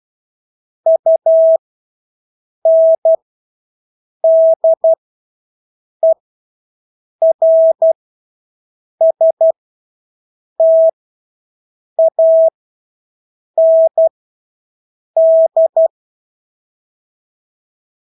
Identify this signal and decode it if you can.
Morse code, heard as a keyed tone, about 12 wpm, spaced as for 6 wpm, 650 Hz. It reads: UNDERSTAND
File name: Engelske ord 12-6 wpm